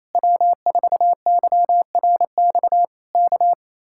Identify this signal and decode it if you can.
Morse code, heard as a keyed tone, about 27 wpm, 700 Hz. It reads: W4YRX K
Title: Speed-Racing Common QSO Phrases - 28wpm